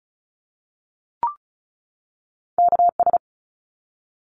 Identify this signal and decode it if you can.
Morse code, heard as a keyed tone, about 34 wpm, 700 Hz. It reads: KS
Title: ARRL Sections 35wpm